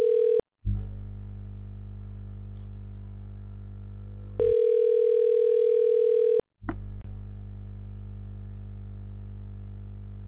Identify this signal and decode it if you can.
Ringback tone (North America), 440 + 480 Hz.
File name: recordings / RE18823ebff19c9a2619e2b7daf5c32346